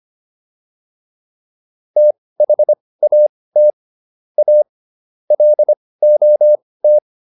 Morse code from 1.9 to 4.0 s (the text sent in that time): THAT